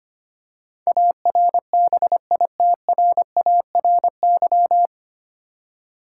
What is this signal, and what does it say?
Morse code, heard as a keyed tone, about 25 wpm, 700 Hz. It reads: ARBITRARY
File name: practice / 25wpm / 9 Letter Words - First 3 Letters Given - 25wpm